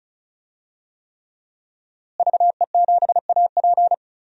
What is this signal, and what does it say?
Morse code, heard as a keyed tone, about 36 wpm, 700 Hz. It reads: VE7AP
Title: DXCC-100 Call Signs - 5 Characters - Repeated 3x - 35wpm